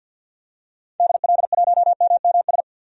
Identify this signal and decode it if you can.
Morse code, heard as a keyed tone, about 47 wpm, 700 Hz. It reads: DL1MMS
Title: DXCC-100 Call Signs - 6 Characters - 50wpm